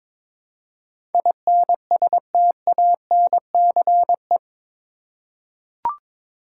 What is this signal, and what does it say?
Morse code, heard as a keyed tone, about 21 wpm, 700 Hz. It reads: INSTANCE
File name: practice / 22wpm / Eight Letter Words - 22wpm